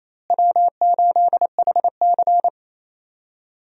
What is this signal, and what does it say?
Morse code, heard as a keyed tone, about 28 wpm, 700 Hz. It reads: W8HC
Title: USA Call Signs - 4 Characters - Repeated 2x - 28wpm